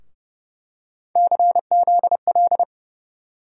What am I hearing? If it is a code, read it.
Morse code, heard as a keyed tone, about 29 wpm, 700 Hz. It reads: CZL
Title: USA Call Sign Postfixes - 3 Letters 30wpm